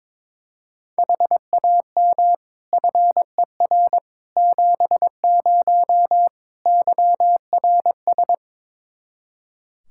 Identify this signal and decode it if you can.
Morse code, heard as a keyed tone, about 22 wpm, 700 Hz. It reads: HAM FER 70 YRS
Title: Practice QSO - Exchange 3 - More Info - 22wpm